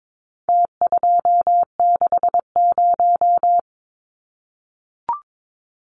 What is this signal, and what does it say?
Morse code, heard as a keyed tone, about 22 wpm, 700 Hz. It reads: T260
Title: Numbers - Contest Style - 4-Digit 22wpm